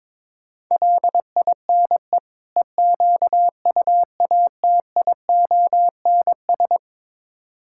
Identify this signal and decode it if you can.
Morse code, heard as a keyed tone, about 22 wpm, 700 Hz. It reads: LINE EQUATIONS
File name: practice / 22wpm / Build A Phrase - 5-Word Phrases - 22wpm